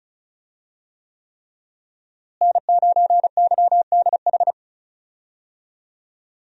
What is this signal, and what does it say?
Morse code, heard as a keyed tone, about 36 wpm, 700 Hz. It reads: N9YDH
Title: USA Call Signs - 5 Characters - Repeated 3x - 35wpm